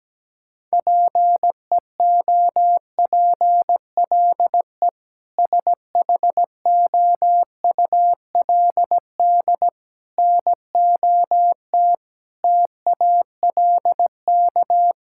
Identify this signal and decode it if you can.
Morse code, heard as a keyed tone, about 17 wpm, 700 Hz. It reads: PEOPLE SHOULD NOT TALK
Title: Sentences from Top 1000 Words - Review Part 6 - 17wpm